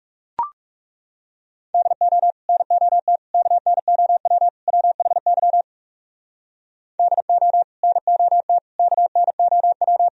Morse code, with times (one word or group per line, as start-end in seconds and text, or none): 1.7-2.3 DO
2.5-3.2 NOT
3.3-4.5 KNOW
4.7-5.6 WHY
7.0-7.6 DO
7.8-8.6 NOT
8.8-10.1 KNOW